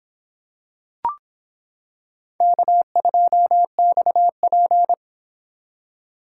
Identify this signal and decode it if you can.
Morse code, heard as a keyed tone, about 26 wpm, 700 Hz. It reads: K2XP